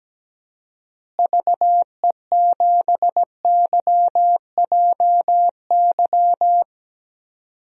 Morse code, heard as a keyed tone, about 17 wpm, 700 Hz. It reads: VE7YJY
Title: DXCC-100 Call Signs - 6 Characters - Repeated 2x - 17wpm